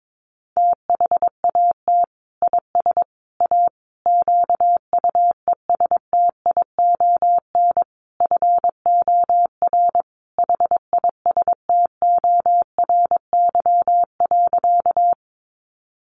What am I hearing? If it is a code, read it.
Morse code, heard as a keyed tone, about 22 wpm, 700 Hz. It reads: THAT IS A QUESTION FOR HISTORY.
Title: Sentences from Top 600 Words 22wpm